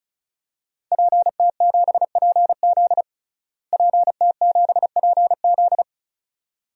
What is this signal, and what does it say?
Morse code, heard as a keyed tone, about 34 wpm, 700 Hz. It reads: PT7PZ PT7PZ
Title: DXCC-100 Call Signs - 5 Characters - Repeated 2x - 35wpm